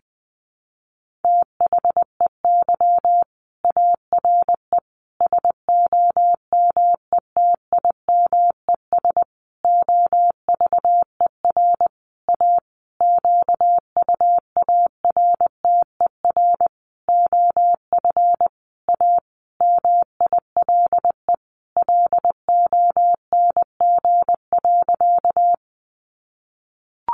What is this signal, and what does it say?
Morse code, heard as a keyed tone, about 20 wpm, 700 Hz. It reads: THEY ARE SOMETIMES OVER A QUARTER OF A MILE LONG.
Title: Sentences from Top 1300 Words 20wpm